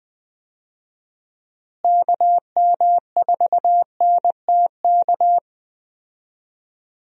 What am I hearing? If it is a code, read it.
Morse code, heard as a keyed tone, about 21 wpm, 700 Hz. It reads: KM4NTK
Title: USA Call Signs - 6 Characters - Repeated 2x - 20wpm